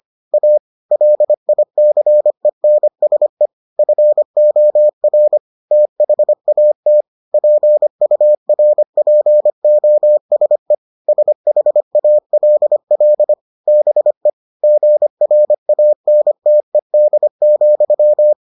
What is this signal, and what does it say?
Morse code, heard as a keyed tone, about 25 wpm, 600 Hz. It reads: A LICENSE FOR THAT PURPOSE SHALL BE GRANTED,